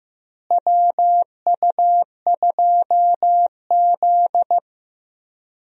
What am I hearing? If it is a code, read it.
Morse code, heard as a keyed tone, about 15 wpm, 700 Hz. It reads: WU2Z
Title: USA Call Signs - 4 Characters - Repeated 3x - 15wpm